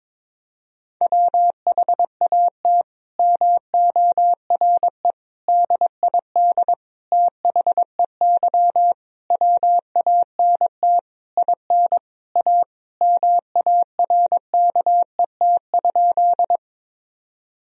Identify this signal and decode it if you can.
Morse code, heard as a keyed tone, about 22 wpm, 700 Hz. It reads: WHAT MORE DID THEY WANT IN A MARKET?
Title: Sentences from Top 1000 Words - Review Part 3 - 22wpm